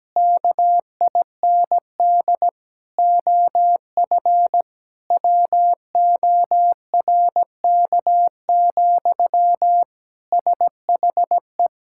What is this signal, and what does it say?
Morse code, heard as a keyed tone, about 17 wpm, 700 Hz. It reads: KIND OF WORK, SHE
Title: Sentences from Top 1000 Words - Review Part 6 - 17wpm